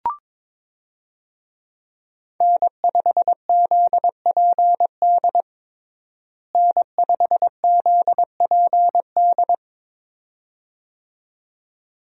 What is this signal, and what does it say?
Morse code, heard as a keyed tone, about 22 wpm, 700 Hz. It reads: N5ZPD N5ZPD
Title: USA Call Signs - 5 Characters - Repeated 2x - 22wpm